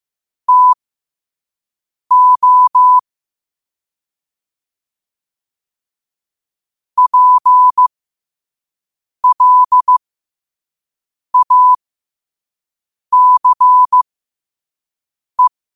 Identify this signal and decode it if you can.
Morse code, heard as a keyed tone, about 15 wpm, spaced as for 5 wpm, 1000 Hz. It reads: TO PLACE
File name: qotdfiles / 5wpm